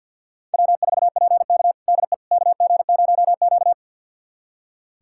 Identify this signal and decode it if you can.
Morse code, heard as a keyed tone, about 47 wpm, 700 Hz. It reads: W4PK DE KG0Q
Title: USA Callsign DE Callsign - 4 Characters - 50wpm